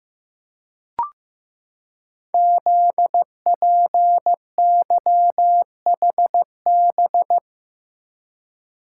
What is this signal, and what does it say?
Morse code, heard as a keyed tone, about 15 wpm, 700 Hz. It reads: ZPYHB
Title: Code Groups - 5 Letter:Numbers - 15wpm